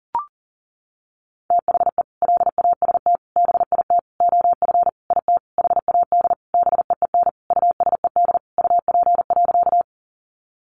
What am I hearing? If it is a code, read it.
Morse code, heard as a keyed tone, about 40 wpm, 700 Hz. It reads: THE LAST BIT OF IT HAD BEEN USED UP.
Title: Sentences from Top 900 Words 40wpm